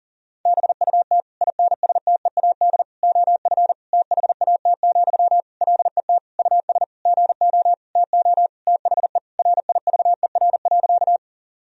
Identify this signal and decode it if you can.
Morse code, heard as a keyed tone, about 40 wpm, 700 Hz. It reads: BUT INSTEAD OF THAT, LET US GO TO THE RIVER.
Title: Sentences from Top 1000 Words - Review Part 4 - 40wpm